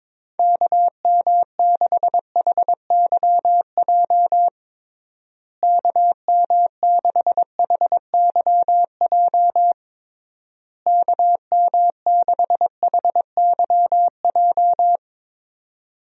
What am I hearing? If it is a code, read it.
Morse code, heard as a keyed tone, about 22 wpm, 700 Hz. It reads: KM6HYJ KM6HYJ KM6HYJ
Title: USA Call Signs - 6 Characters - Repeated 3x - 22wpm